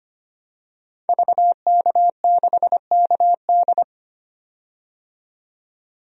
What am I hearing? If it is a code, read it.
Morse code, heard as a keyed tone, about 25 wpm, 700 Hz. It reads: VK6KD
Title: DXCC-100 Call Signs - 5 Characters - 25wpm